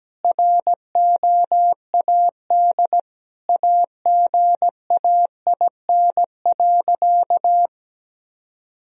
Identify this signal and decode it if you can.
Morse code, heard as a keyed tone, about 17 wpm, 700 Hz. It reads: ROAD AGAIN.
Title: Sentences from Top 1000 Words - Review Part 2 - 17wpm